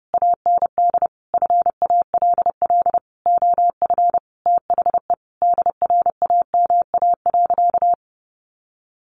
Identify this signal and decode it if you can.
Morse code, heard as a keyed tone, about 30 wpm, 700 Hz. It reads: AND FALL OF THE DRAMA.
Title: Sentences from Top 1500 Words 30wpm